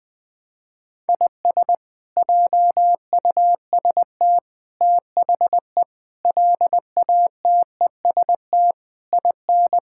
Morse code, heard as a keyed tone, about 21 wpm, 700 Hz. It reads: IS JUST THE LATEST IN
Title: Sets of 8 Words - 20wpm